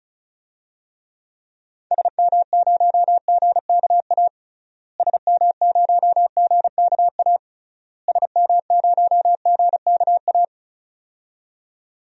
Morse code, heard as a keyed tone, about 36 wpm, 700 Hz. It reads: SM0GKA SM0GKA SM0GKA
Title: DXCC-100 Call Signs - 6 Characters - Repeated 3x - 35wpm